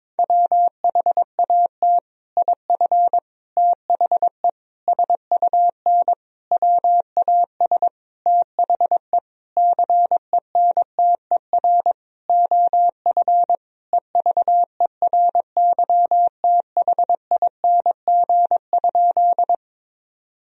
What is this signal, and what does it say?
Morse code, heard as a keyed tone, about 22 wpm, 700 Hz. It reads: WHAT IF THE SUN WAS THE CENTER OF EVERYTHING?
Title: Sentences from Top 1000 Words - Review Part 2 - 22wpm